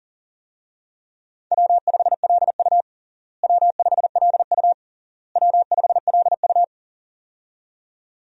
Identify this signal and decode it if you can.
Morse code, heard as a keyed tone, about 40 wpm, 700 Hz. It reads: W5LU W5LU W5LU
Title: USA Call Signs - 4 Characters - Repeated 3x - 40wpm